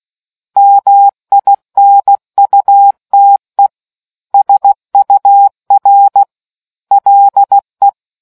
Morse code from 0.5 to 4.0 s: MINUTE

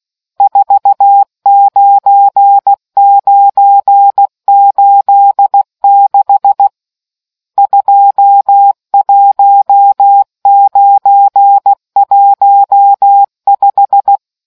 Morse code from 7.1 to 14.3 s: 21915

